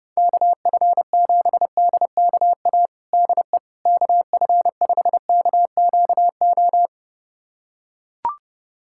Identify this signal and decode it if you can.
Morse code, heard as a keyed tone, about 30 wpm, 700 Hz. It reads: KF7DKA DE KF5KQO
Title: USA Callsign DE Callsign - 6 Characters - 30wpm